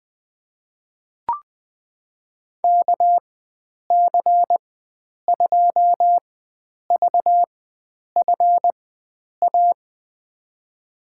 Morse code, heard as a keyed tone, about 21 wpm, spaced as for 9 wpm, 700 Hz. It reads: KC2VFA
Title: USA Call Signs - 6 Characters - 4x Character Spacing - 20wpm